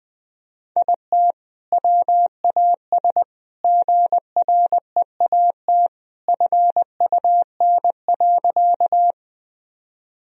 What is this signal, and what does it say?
Morse code, heard as a keyed tone, about 20 wpm, 700 Hz. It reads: IT WAS GREAT FUN.
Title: Sentences from Top 1000 Words - Review Part 4 - 20wpm